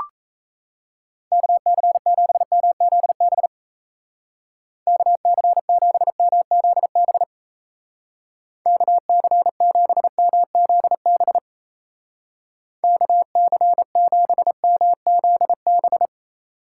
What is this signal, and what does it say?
Morse code, heard as a keyed tone, about 36 wpm, 700 Hz. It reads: KC7MZB KC7MZB KC7MZB KC7MZB